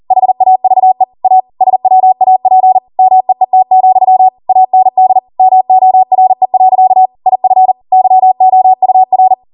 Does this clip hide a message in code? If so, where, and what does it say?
Yes: Morse code from 0.1 to 9.3 s, heard as a keyed tone, about 40 wpm, 750 Hz. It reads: HAVE A SWAP MEET, AND MORE. IF YOUR